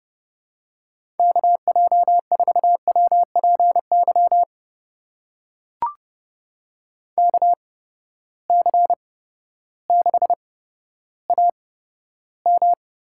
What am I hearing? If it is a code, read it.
Morse code, heard as a keyed tone, about 30 wpm, spaced as for 14 wpm, 700 Hz. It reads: KJ4WPY KC6AM
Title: USA Call Signs - 8x Character Spacing - 30wpm